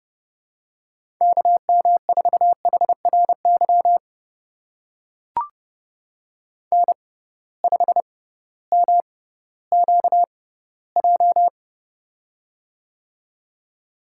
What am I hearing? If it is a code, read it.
Morse code, heard as a keyed tone, about 29 wpm, spaced as for 16 wpm, 700 Hz. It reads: KM4HRY N5MQJ
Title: USA Call Signs - 6x Character Spacing - 30wpm